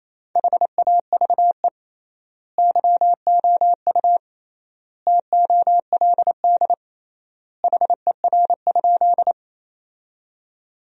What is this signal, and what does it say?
Morse code, heard as a keyed tone, about 28 wpm, 700 Hz. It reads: HAVE YOU TOLD HER?
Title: 3x Word-Spacing - Sentences from Top 400 Words - 28wpm